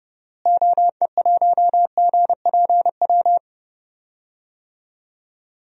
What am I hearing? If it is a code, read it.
Morse code, heard as a keyed tone, about 30 wpm, 700 Hz. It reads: OE1GPW
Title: DXCC-100 Call Signs - 6 Characters - Repeated 3x - 30wpm